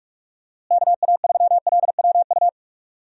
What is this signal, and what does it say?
Morse code, heard as a keyed tone, about 46 wpm, 700 Hz. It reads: KA3LWU
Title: USA Call Signs - 6 Characters - Repeated 2x - 45wpm